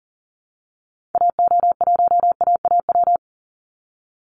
Morse code, heard as a keyed tone, about 40 wpm, 700 Hz. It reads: AO1AAW
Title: Call Signs - 40wpm